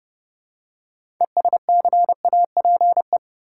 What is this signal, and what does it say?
Morse code, heard as a keyed tone, about 30 wpm, 700 Hz. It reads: ESCAPE
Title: Six Letter Words - 30wpm